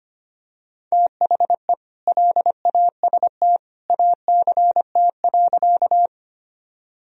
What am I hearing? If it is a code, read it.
Morse code, heard as a keyed tone, about 25 wpm, 700 Hz. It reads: THE LAST ACT.
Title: Sentences from Top 1000 Words - Review Part 1 - 25wpm